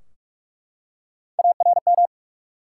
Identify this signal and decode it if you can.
Morse code, heard as a keyed tone, about 44 wpm, 700 Hz. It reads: ARM
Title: Three Letter Words - 45wpm